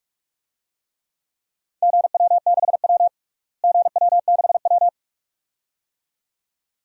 Morse code, heard as a keyed tone, about 46 wpm, 700 Hz. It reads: GW6W GW6W
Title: DXCC-100 Call Signs - 4 Characters - Repeated 2x - 45wpm